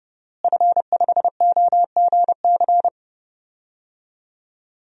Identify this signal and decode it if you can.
Morse code, heard as a keyed tone, about 31 wpm, 700 Hz. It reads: F5OGC
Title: DXCC-100 Call Signs - 5 Characters - Repeated 2x - 30wpm